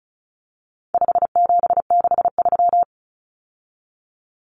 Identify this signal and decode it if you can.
Morse code, heard as a keyed tone, about 34 wpm, 700 Hz. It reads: 5763
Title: Numbers - 4-Digit 35wpm